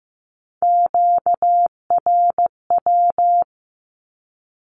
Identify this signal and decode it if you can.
Morse code, heard as a keyed tone, about 15 wpm, 700 Hz. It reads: QRW
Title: Q-Codes 15wpm